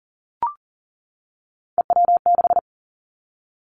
Morse code, heard as a keyed tone, about 38 wpm, 690 Hz. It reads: EW6
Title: Three Challenging Letter-Number Combinations 40wpm